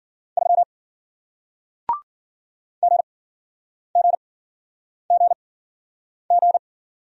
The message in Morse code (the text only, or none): 3 GGGG